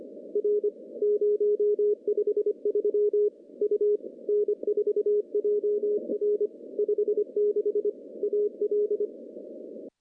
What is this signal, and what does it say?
Morse code, heard as a keyed tone, about 22 wpm, 410 Hz. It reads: R 053 U N4JR 56 AL
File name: contests / CW-SS-k5tr- -0100